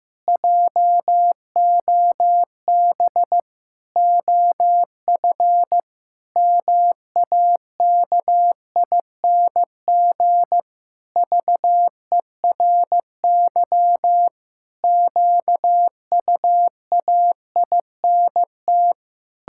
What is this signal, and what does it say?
Morse code, heard as a keyed tone, about 15 wpm, 700 Hz. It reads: JOB OF MAKING VERY QUAINT
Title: Pangram Challenge v2 - 15wpm